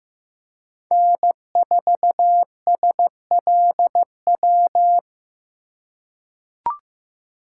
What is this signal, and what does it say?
Morse code, heard as a keyed tone, about 15 wpm, 700 Hz. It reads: N4SLW
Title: USA Call Signs - 2x Character Spacing - 15wpm